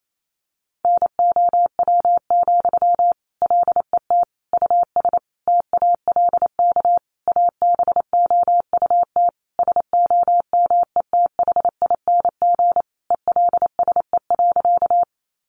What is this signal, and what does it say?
Morse code, heard as a keyed tone, about 28 wpm, 700 Hz. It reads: NOW, LET US TALK ABOUT SOMETHING ELSE.
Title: Sentences from Top 800 Words 28wpm